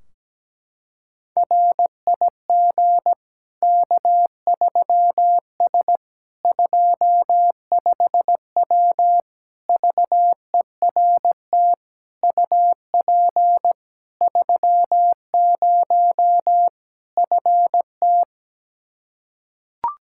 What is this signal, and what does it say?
Morse code, heard as a keyed tone, about 17 wpm, 700 Hz. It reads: RIG K3S 25W VERT UP 30 FT
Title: Practice QSO - Exchange 2 - Info - 17wpm